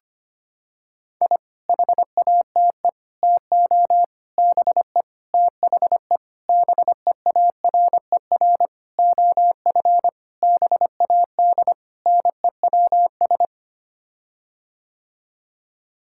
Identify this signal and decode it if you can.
Morse code, heard as a keyed tone, about 25 wpm, 700 Hz. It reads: I HATE TO BE THE BEARER OF BAD NEWS
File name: Sets of 9 Words - 25wpm